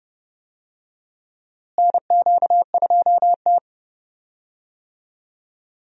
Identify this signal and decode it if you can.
Morse code, heard as a keyed tone, about 30 wpm, 700 Hz. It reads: NQ2T
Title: USA Call Signs - 4 Characters - Repeated 3x - 30wpm